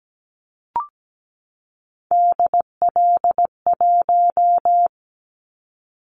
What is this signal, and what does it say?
Morse code, heard as a keyed tone, about 17 wpm, 700 Hz. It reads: DL1
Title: Callsign Prefixes for Top 25 DXCC - 17wpm